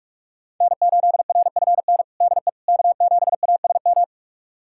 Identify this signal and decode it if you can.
Morse code, heard as a keyed tone, about 46 wpm, 700 Hz. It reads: N8RFN DE K7ASM
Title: USA Callsign DE Callsign - 5 Characters - 45wpm